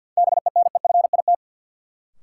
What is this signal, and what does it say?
Morse code, heard as a keyed tone, about 50 wpm, 700 Hz. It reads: BENEFIT
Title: Sets of 5 Words Encore - New Word Drill - 50wpm